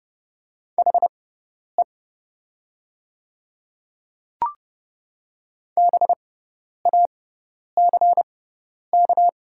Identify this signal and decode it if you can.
Morse code, heard as a keyed tone, about 30 wpm, spaced as for 10 wpm, 700 Hz. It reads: HE BACK
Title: 6x Character Spacing - Top 200 Words - 30wpm